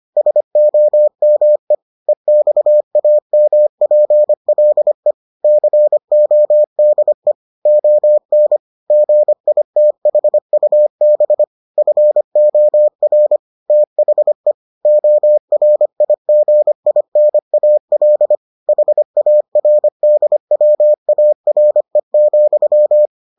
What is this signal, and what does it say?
Morse code, heard as a keyed tone, about 25 wpm, 600 Hz. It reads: SOME EXAMPLE CODE ON GITHUB FOR THE ORIGINAL HARDWARE,